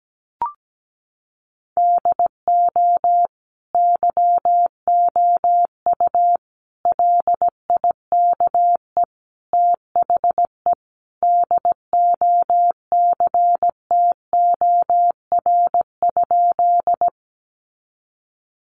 Morse code, heard as a keyed tone, about 17 wpm, 700 Hz. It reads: DO YOU LIKE THE DOCTOR?
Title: Sentences from Top 1500 Words 17wpm